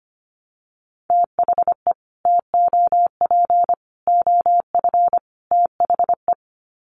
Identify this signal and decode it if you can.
Morse code, heard as a keyed tone, about 25 wpm, 700 Hz. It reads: THE TOP OF THE
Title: Sets of 4 Words 25wpm